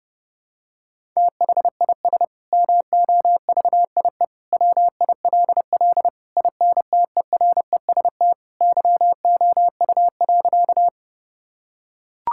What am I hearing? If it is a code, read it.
Morse code, heard as a keyed tone, about 30 wpm, 700 Hz. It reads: THIS MOVIE WILL INTEREST YOU.
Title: Sentences from Top 1000 Words - Review Part 5 - 30wpm